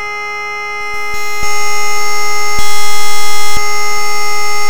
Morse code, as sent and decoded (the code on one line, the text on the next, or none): none